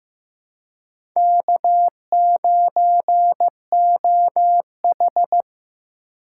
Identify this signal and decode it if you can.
Morse code, heard as a keyed tone, about 15 wpm, 700 Hz. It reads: K9OH
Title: USA Call Signs - 4 Characters - 3x Character Spacing - 15wpm